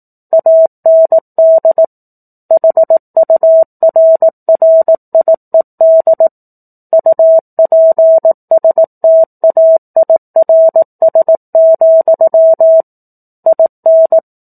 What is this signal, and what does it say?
Morse code, heard as a keyed tone, about 18 wpm, 650 Hz. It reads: AND HURRIED UPSTAIRS, IN
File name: 04-Chapter-04 18 WPM